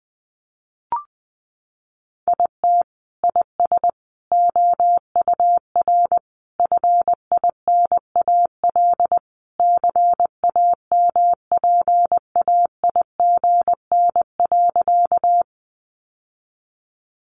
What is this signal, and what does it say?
Morse code, heard as a keyed tone, about 20 wpm, 700 Hz. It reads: IT IS OUR FINAL CAMPAIGN.
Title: Sentences from Top 1200 Words 20wpm